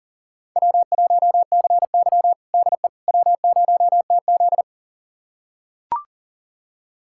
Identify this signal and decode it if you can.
Morse code, heard as a keyed tone, about 40 wpm, 700 Hz. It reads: W1CY DE W0TZ
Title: USA Callsign DE Callsign - 4 Characters - 40wpm